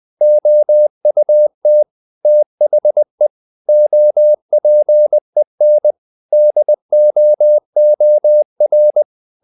Morse code, heard as a keyed tone, about 20 wpm, 600 Hz. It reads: OUT THE OPEN DOOR
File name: the old man and the sea / the old man and the sea part 1 20wpm